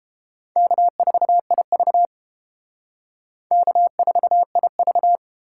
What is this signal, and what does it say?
Morse code, heard as a keyed tone, about 29 wpm, 700 Hz. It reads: K4IV K4IV